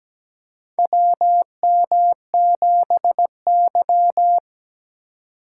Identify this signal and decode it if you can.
Morse code, heard as a keyed tone, about 17 wpm, 700 Hz. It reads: WM7Y
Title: USA Call Signs - 4 Characters - 3x Character Spacing - 17wpm